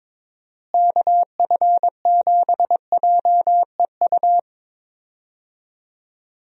Morse code, heard as a keyed tone, about 22 wpm, 700 Hz. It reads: KF7JEU
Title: USA Call Signs - 6 Characters - Repeated 3x - 22wpm